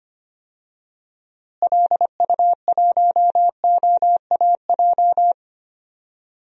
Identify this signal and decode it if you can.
Morse code, heard as a keyed tone, about 25 wpm, 700 Hz. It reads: LU1OAJ
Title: DXCC-100 Call Signs - 6 Characters - Repeated 2x - 25wpm